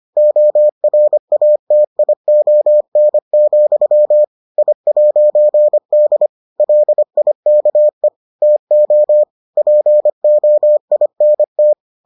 Morse code, heard as a keyed tone, about 25 wpm, 600 Hz. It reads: ORATION, I'D LIKE TO POINT